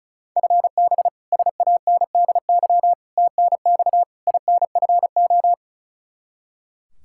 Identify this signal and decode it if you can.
Morse code, heard as a keyed tone, about 34 wpm, 700 Hz. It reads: FB SANDY TNX INFO
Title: Practice QSO - Exchange 3 - More Info - 35wpm